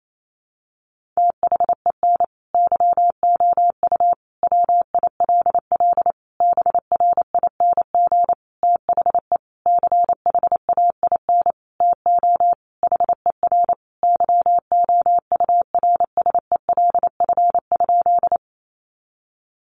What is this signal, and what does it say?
Morse code, heard as a keyed tone, about 28 wpm, 700 Hz. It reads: THEN YOU WILL BRING THE CHAIN TO HER YOURSELF?
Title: Sentences from Top 1500 Words 28wpm